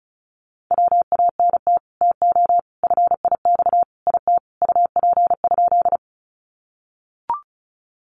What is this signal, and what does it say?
Morse code, heard as a keyed tone, about 34 wpm, 700 Hz. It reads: WANT TO FIX IT UP?
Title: Sentences from Top 1400 Words 35wpm